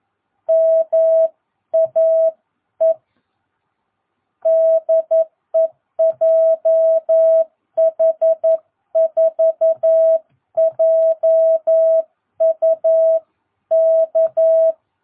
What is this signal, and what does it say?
Morse code, heard as a keyed tone, about 11 wpm, 660 Hz. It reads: MAE DE JH4JUK
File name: jh8cbh / 2022cw